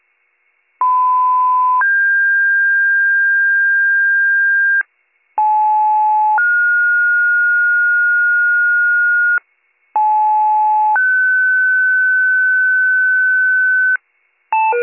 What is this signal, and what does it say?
Two-tone page, heard as a one-second tone, then a three-second tone, about 1010 then 1600 Hz, about 860 then 1430 Hz, about 860 then 1500 Hz. It is heard from 0.8 to 14.0 s.